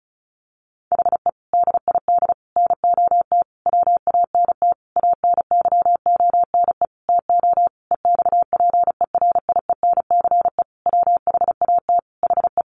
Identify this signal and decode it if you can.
Morse code, heard as a keyed tone, about 34 wpm, 700 Hz. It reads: HE DID NOT WANT ANYONE TO EXPERIENCE WHAT HE
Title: Sentences from Top 1100 Words 35wpm